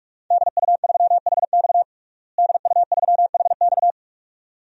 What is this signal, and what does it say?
Morse code, heard as a keyed tone, about 46 wpm, 700 Hz. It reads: DU3HX DU3HX
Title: DXCC-100 Call Signs - 5 Characters - Repeated 2x - 45wpm